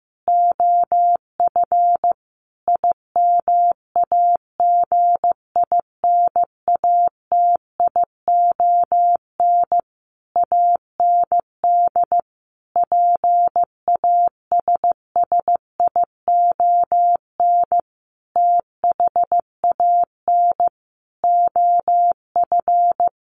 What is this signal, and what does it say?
Morse code, heard as a keyed tone, about 15 wpm, 700 Hz. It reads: OF IMAGINATION AND PASSION THAN OF
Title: Sentences from Top 1300 Words 15wpm